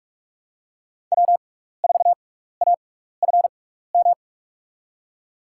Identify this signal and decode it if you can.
Morse code, heard as a keyed tone, about 44 wpm, spaced as for 15 wpm, 690 Hz. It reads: W4AFM
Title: USA Call Signs - 5 Characters - 6x Character Spacing - 45wpm